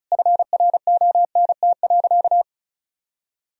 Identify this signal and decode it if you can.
Morse code, heard as a keyed tone, about 36 wpm, 700 Hz. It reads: FRONT.
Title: Sentences from Top 1000 Words - Review Part 2 - 35wpm